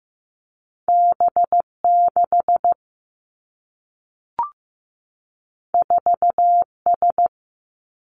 Morse code, heard as a keyed tone, about 15 wpm, 700 Hz. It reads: B6 4S
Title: Two Challenging Letter-Number Combinations 15wpm